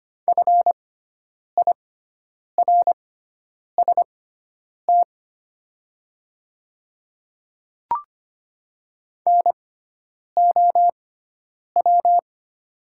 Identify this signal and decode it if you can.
Morse code, heard as a keyed tone, about 25 wpm, spaced as for 9 wpm, 700 Hz. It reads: FIRST NOW
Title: 6x Character Spacing - Top 100 Words - 25wpm